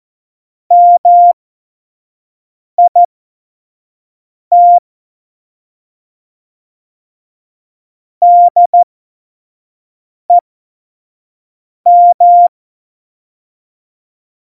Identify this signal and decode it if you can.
Morse code, heard as a keyed tone, about 14 wpm, spaced as for 5 wpm, 700 Hz. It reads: MIT DEM
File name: Morsekurs / Witze / Witz 26. Februar 2019, 25 F70